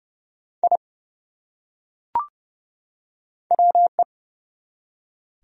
Morse code, heard as a keyed tone, about 30 wpm, 700 Hz. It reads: I WE
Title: Top 100 Words 30wpm